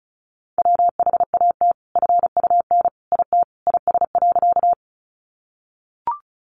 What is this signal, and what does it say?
Morse code, heard as a keyed tone, about 34 wpm, 700 Hz. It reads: WHAT FUN IT IS.
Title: Sentences from Top 700 Words 35wpm